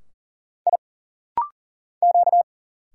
Morse code, heard as a keyed tone, about 40 wpm, 700 Hz. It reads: I Q
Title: Single Letters - Rapid-Fire 40wpm